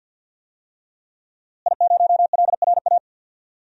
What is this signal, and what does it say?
Morse code, heard as a keyed tone, about 52 wpm, 700 Hz. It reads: I0LRA
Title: DXCC-100 Call Signs - 5 Characters - Repeated 3x - 50wpm